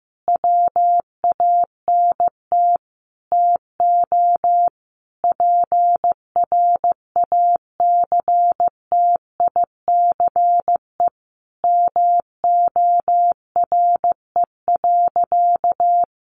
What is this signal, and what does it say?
Morse code, heard as a keyed tone, about 15 wpm, 700 Hz. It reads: WANT TO PRACTICE MORE.
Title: Sentences from Top 700 Words 15wpm